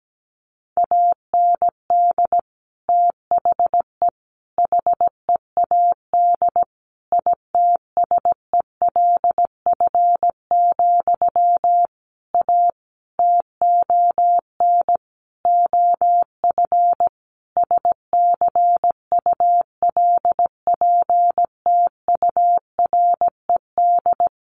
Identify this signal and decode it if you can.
Morse code, heard as a keyed tone, about 17 wpm, 700 Hz. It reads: AND THE HEAD ITSELF, A TON OF SCULPTURED
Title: A Sound Of Thunder 17wpm